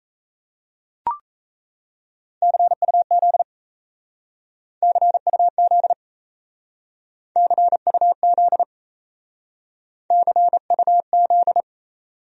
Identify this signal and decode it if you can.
Morse code, heard as a keyed tone, about 35 wpm, 700 Hz. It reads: CUZ CUZ CUZ CUZ
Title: Speed-Racing Abbreviations - 28wpm